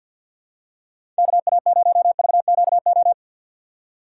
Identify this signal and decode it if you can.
Morse code, heard as a keyed tone, about 51 wpm, 700 Hz. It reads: KA0VQO
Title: USA Call Signs - 6 Characters - 4x Character Spacing - 50wpm